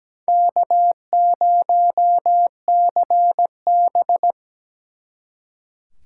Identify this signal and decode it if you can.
Morse code, heard as a keyed tone, about 20 wpm, 700 Hz. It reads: K0CB
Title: Speed Racing - USA 1x2 Call Signs 17wpm